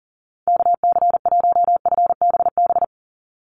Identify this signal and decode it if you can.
Morse code, heard as a keyed tone, about 40 wpm, 700 Hz. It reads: KC1FBB
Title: USA Call Signs 40wpm